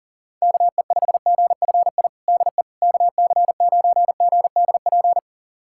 Morse code, heard as a keyed tone, about 40 wpm, 700 Hz. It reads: KE5GFI DE KC9GDP
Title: USA Callsign DE Callsign 40wpm